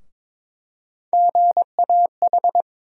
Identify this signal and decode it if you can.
Morse code, heard as a keyed tone, about 22 wpm, 700 Hz. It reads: GAH
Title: USA Call Sign Postfixes - 3 Letters 22wpm